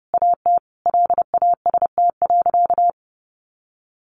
Morse code, heard as a keyed tone, about 30 wpm, 700 Hz. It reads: AT LAST.
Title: Sentences from Top 1100 Words 30wpm